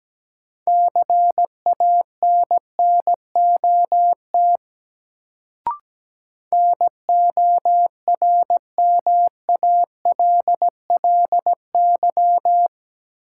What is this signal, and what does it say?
Morse code, heard as a keyed tone, about 17 wpm, 700 Hz. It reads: CANNOT NORMALLY
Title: Sentences from Top 1000 Words - Review Part 6 - New Word Drill - 17wpm